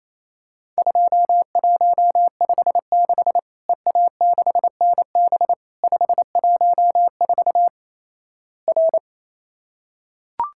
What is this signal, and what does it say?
Morse code, heard as a keyed tone, about 27 wpm, 700 Hz. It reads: 2156 EA6NB 514 R
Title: Worked All Europe DX Contest - QTC Exchange - 28wpm